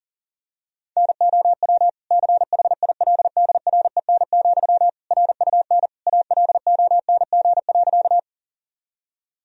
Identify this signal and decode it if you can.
Morse code, heard as a keyed tone, about 40 wpm, 700 Hz. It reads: NOW CHILDREN, RUN ALONG.
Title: Sentences from Top 1000 Words - Review Part 1 - 40wpm